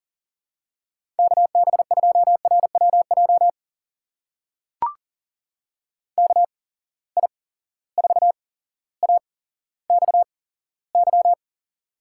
Morse code, heard as a keyed tone, about 40 wpm, spaced as for 17 wpm, 700 Hz. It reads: KB2RWJ KI4AXY
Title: USA Call Signs - 6 Characters - 8x Character Spacing - 40wpm